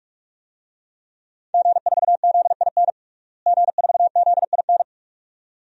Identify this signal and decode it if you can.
Morse code, heard as a keyed tone, about 46 wpm, 700 Hz. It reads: G4ZIN G4ZIN
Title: DXCC-100 Call Signs - 5 Characters - Repeated 2x - 45wpm